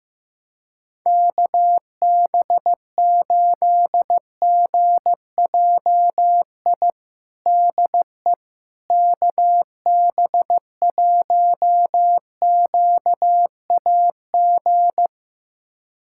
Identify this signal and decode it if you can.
Morse code, heard as a keyed tone, about 15 wpm, 700 Hz. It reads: KB8GJI DE KB1QAG
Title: USA Callsign DE Callsign 15wpm